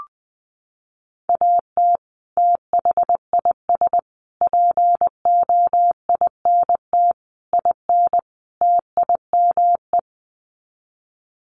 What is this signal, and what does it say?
Morse code, heard as a keyed tone, about 20 wpm, 700 Hz. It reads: AT THIS POINT IN TIME
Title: Sets of 5 Words Encore 20wpm